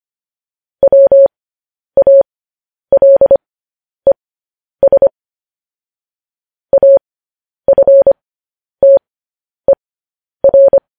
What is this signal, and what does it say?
Morse code, heard as a keyed tone, about 25 wpm, spaced as for 10 wpm, 550 Hz. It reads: WALES AFTER
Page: news-headlines-in-morse-code-at-10-wpm